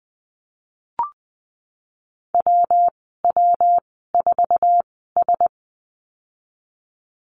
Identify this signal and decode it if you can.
Morse code, heard as a keyed tone, about 20 wpm, spaced as for 15 wpm, 700 Hz. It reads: WW4S
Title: 173 Lesson 28g - Number 9 - Call Signs 20wpm